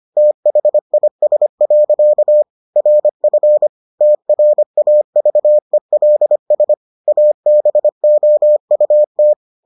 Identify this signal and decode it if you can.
Morse code, heard as a keyed tone, about 25 wpm, 600 Hz. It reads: THIS. RF TRAVELS ABOUT